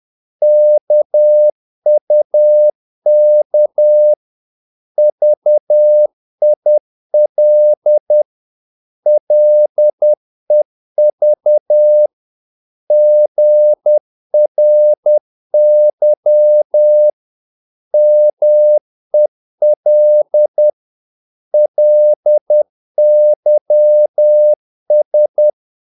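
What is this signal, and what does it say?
Morse code, heard as a keyed tone, about 10 wpm, 600 Hz. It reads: KUK VIL LEV GRY MEL LYS
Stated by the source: Morse practice of short Danish words at 10 wpm, from CW med Gnister.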